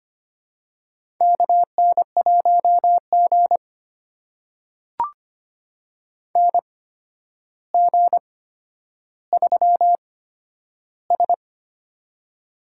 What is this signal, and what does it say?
Morse code, heard as a keyed tone, about 25 wpm, spaced as for 11 wpm, 700 Hz. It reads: KN1G NG3S
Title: USA Call Signs - 4 Characters - 8x Character Spacing - 25wpm